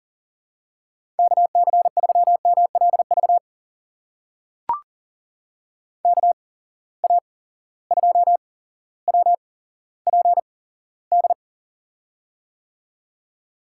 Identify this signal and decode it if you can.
Morse code, heard as a keyed tone, about 40 wpm, spaced as for 17 wpm, 700 Hz. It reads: KC3MLV KA2WPD
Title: USA Call Signs - 6 Characters - 8x Character Spacing - 40wpm